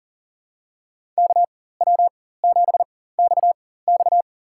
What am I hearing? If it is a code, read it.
Morse code, heard as a keyed tone, about 40 wpm, spaced as for 19 wpm, 700 Hz. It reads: KW7XX
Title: USA Call Signs - 5 Characters - 4x Character Spacing - 40wpm